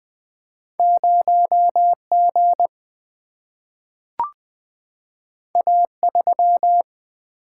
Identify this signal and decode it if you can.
Morse code, heard as a keyed tone, about 20 wpm, 700 Hz. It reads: 0 G A 3
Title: Single-Letter Single-Number 20wpm